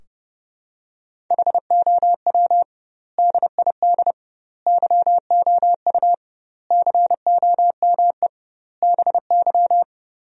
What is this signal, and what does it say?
Morse code, heard as a keyed tone, about 30 wpm, 700 Hz. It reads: HOW DID YOU COME BY
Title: 2x Word-Spacing - Sentences from Top 100 Words - 30wpm